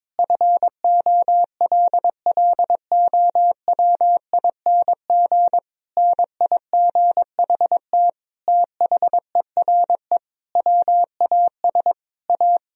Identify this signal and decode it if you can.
Morse code, heard as a keyed tone, about 22 wpm, 700 Hz. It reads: FOLLOWING NIGHT THERE WAS A
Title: Sentences from Top 1000 Words - Review Part 4 - 22wpm